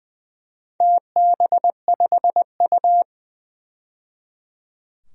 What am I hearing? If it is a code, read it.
Morse code, heard as a keyed tone, about 20 wpm, 700 Hz. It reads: TB5U
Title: DXCC-100 Call Signs - 4 Characters - 20wpm